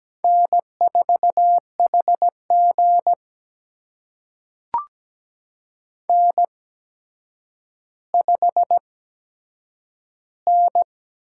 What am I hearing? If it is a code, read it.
Morse code, heard as a keyed tone, about 17 wpm, spaced as for 8 wpm, 700 Hz. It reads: N4HG N5N
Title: USA Call Signs - 4 Characters - 8x Character Spacing - 17wpm